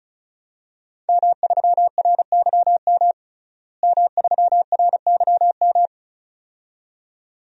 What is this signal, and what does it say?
Morse code, heard as a keyed tone, about 36 wpm, 700 Hz. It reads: M3RYM M3RYM
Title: DXCC-100 Call Signs - 5 Characters - Repeated 2x - 35wpm